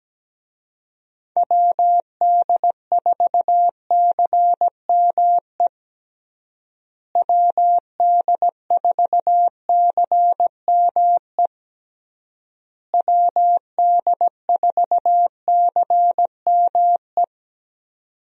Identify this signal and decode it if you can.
Morse code, heard as a keyed tone, about 17 wpm, 700 Hz. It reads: WD4CME WD4CME WD4CME
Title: USA Call Signs - 6 Characters - Repeated 3x - 17wpm